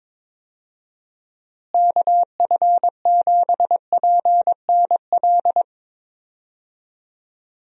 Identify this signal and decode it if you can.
Morse code, heard as a keyed tone, about 22 wpm, 700 Hz. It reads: KF7PNL
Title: USA Call Signs - 6 Characters - Repeated 2x - 22wpm